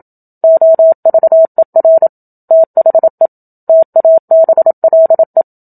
Morse code, heard as a keyed tone, about 27 wpm, 650 Hz. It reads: OVER THE TABLE
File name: Book Two / 01-Chapter 1 - 27 WPM